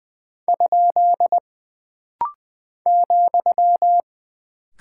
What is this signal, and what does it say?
Morse code, heard as a keyed tone, about 20 wpm, 700 Hz. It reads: ? ,
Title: Single Punctuation-Prosign - Rapid-Fire - 20wpm